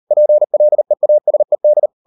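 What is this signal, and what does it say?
Morse code, heard as a keyed tone, about 38 wpm, 600 Hz. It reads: PLEASED
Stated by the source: Morse code practice at 38 WPM, sending Ken Follett's novel Never in English, from CW med Gnister.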